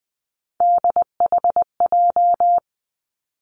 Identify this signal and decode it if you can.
Morse code, heard as a keyed tone, about 20 wpm, 700 Hz. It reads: DHJ
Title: Three Challenging Letter-Number Combinations 20wpm